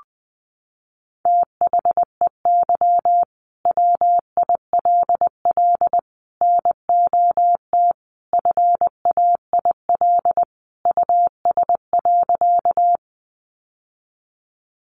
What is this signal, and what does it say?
Morse code, heard as a keyed tone, about 20 wpm, 700 Hz. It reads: THEY WILL NOT FAIL US.
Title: Sentences from Top 1100 Words 20wpm